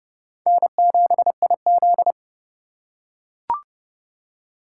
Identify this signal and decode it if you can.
Morse code, heard as a keyed tone, about 29 wpm, 700 Hz. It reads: N7IZ